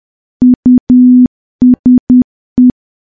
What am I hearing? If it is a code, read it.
Morse code, heard as a keyed tone, about 10 wpm, 250 Hz. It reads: USE
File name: morse / morse-training-10WPM